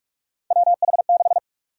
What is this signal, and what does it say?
Morse code, heard as a keyed tone, about 46 wpm, 700 Hz. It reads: WH6
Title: USA Call Sign Prefixes - Repeated 2x - 45wpm